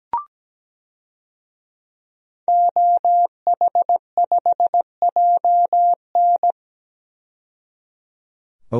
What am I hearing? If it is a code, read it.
Morse code, heard as a keyed tone, about 17 wpm, 700 Hz. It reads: OH5JN
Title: DXCC-100 Call Signs - 5 Characters - 17wpm